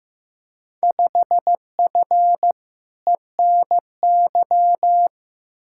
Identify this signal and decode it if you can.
Morse code, heard as a keyed tone, about 15 wpm, 700 Hz. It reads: 5F ENY
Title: Field Day Exchanges Encore - 15wpm